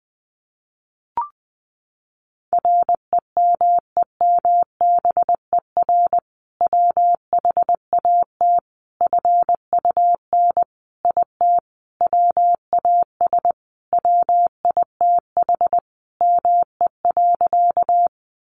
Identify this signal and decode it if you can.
Morse code, heard as a keyed tone, about 20 wpm, 700 Hz. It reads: REMEMBER WHAT FUN IT WAS WITH ME.
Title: Sentences from Top 700 Words 20wpm